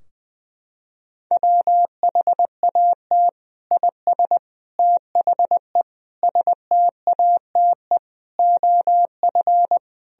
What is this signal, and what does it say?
Morse code, heard as a keyed tone, about 20 wpm, 700 Hz. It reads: WHAT IS THE STATE OF
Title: Sentences from Top 1000 Words - Review Part 3 - 20wpm